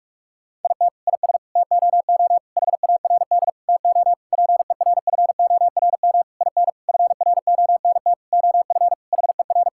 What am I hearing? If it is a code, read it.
Morse code, heard as a keyed tone, about 46 wpm, 700 Hz. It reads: IT IS TOO HARD TO PERFORM IN FRONT OF HER
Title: Sentences from Top 1000 Words - Review Part 5 - 45wpm